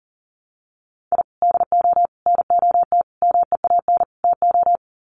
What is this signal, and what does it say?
Morse code, heard as a keyed tone, about 38 wpm, 700 Hz. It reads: I DO NOT MEAN TO
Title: Sets of 5 Words 40wpm